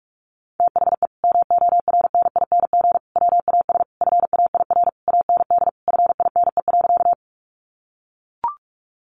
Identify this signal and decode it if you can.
Morse code, heard as a keyed tone, about 44 wpm, 700 Hz. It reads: THE MORNING WAS FAIR AND FINE.
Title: Sentences from Top 1300 Words 45wpm